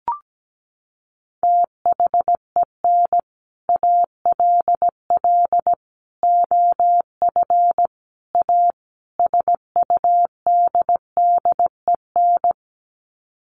Morse code, heard as a keyed tone, about 17 wpm, 700 Hz. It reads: THEN ALL OF A SUDDEN
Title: Sets of 5 Words Encore 17wpm